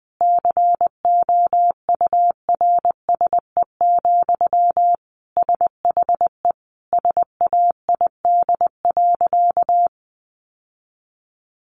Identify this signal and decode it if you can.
Morse code, heard as a keyed tone, about 20 wpm, 700 Hz. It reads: COURSE, SHE SAID.
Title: Sentences from Top 1400 Words 20wpm